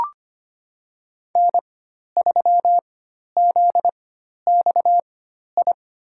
Morse code, heard as a keyed tone, about 25 wpm, spaced as for 12 wpm, 700 Hz. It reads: N3ZXI
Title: USA Call Signs - 5 Characters - 4x Character Spacing - 25wpm